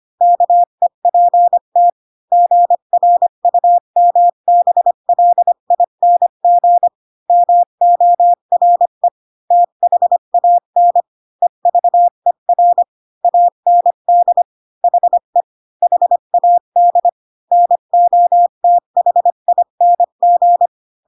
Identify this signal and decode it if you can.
Morse code, heard as a keyed tone, about 25 wpm, 700 Hz. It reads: KEPT GRUMBLING MORE THAN EVER AND HE HAD NOTHING
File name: Pinocchio / Pinocchio ch5 20 25 ik1hge